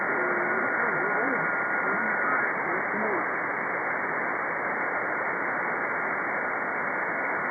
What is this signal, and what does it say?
FT8, with speech.